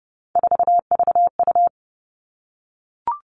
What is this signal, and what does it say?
Morse code, heard as a keyed tone, about 30 wpm, 700 Hz. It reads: 4VU
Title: Three Challenging Letter-Number Combinations 30wpm